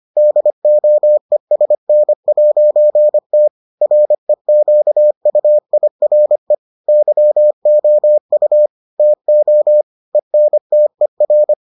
Morse code, heard as a keyed tone, about 25 wpm, 600 Hz. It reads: DOESN'T REQUIRE YOU TO ENTER